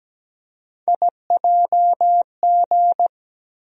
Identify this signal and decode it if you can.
Morse code, heard as a keyed tone, about 17 wpm, 700 Hz. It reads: IJG
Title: USA Call Sign Postfixes - 3 Letters 17wpm